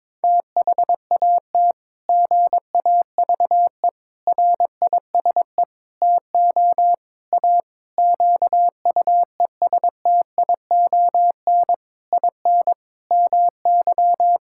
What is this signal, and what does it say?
Morse code, heard as a keyed tone, about 22 wpm, 700 Hz. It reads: THAT GAVE RISE TO A QUESTION IN MY
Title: Sentences from Top 1000 Words - Review Part 6 - 22wpm